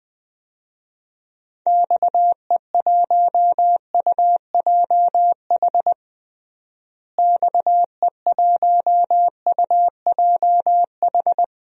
Morse code, heard as a keyed tone, about 20 wpm, 700 Hz. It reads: XE1UJH XE1UJH
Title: DXCC-100 Call Signs - 6 Characters - Repeated 3x - 20wpm